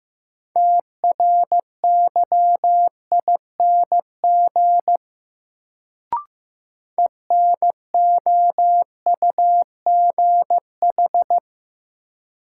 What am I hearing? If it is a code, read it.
Morse code, heard as a keyed tone, about 15 wpm, 700 Hz. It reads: TRYING ENOUGH
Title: Sets of 2 Words - New Word Drill - 15wpm